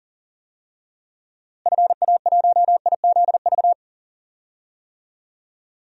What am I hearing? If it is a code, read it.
Morse code, heard as a keyed tone, about 37 wpm, 700 Hz. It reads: FA1IZV
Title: DXCC-100 Call Signs - 6 Characters - Repeated 3x - 40wpm